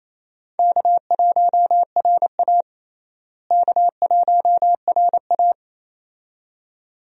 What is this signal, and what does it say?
Morse code, heard as a keyed tone, about 28 wpm, 700 Hz. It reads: K1RA K1RA
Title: USA Call Signs - 4 Characters - Repeated 2x - 28wpm